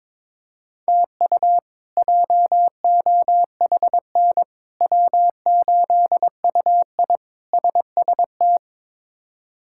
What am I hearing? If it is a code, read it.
Morse code, heard as a keyed tone, about 22 wpm, 700 Hz. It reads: TU JOHN W8UI SST
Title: K1USN SST QSO Practice - 22wpm